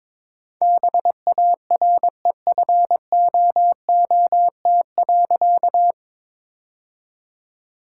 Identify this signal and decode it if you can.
Morse code, heard as a keyed tone, about 22 wpm, 700 Hz. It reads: BAREFOOT.
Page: morse-code-ninja-practice-22wpm